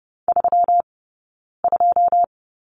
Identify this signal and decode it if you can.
Morse code, heard as a keyed tone, about 30 wpm, 700 Hz. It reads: 3 2
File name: Lesson 20d - Number 3 - Two Characters 30wpm